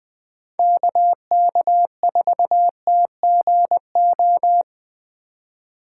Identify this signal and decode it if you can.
Morse code, heard as a keyed tone, about 20 wpm, 700 Hz. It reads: KK4TGO
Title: USA Call Signs - 6 Characters - 3x Character Spacing - 20wpm